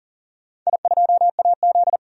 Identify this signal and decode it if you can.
Morse code, heard as a keyed tone, about 38 wpm, 700 Hz. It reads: I2AZ
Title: DXCC-100 Call Signs - 4 Characters - 40wpm